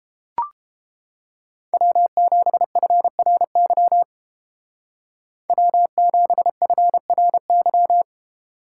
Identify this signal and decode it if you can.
Morse code, heard as a keyed tone, about 31 wpm, 700 Hz. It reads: W7FRY W7FRY